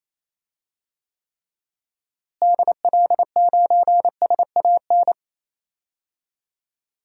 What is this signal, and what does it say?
Morse code, heard as a keyed tone, about 28 wpm, 700 Hz. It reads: DL9SAN
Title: DXCC-100 Call Signs - 6 Characters - Repeated 2x - 28wpm